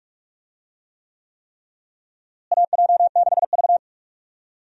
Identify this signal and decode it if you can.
Morse code, heard as a keyed tone, about 45 wpm, 700 Hz. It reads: AJ6V
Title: USA Call Signs - 4 Characters - Repeated 3x - 45wpm